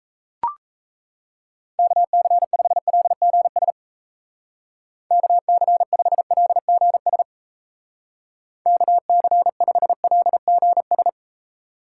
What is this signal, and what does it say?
Morse code, heard as a keyed tone, about 35 wpm, 700 Hz. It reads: KC5LGS KC5LGS KC5LGS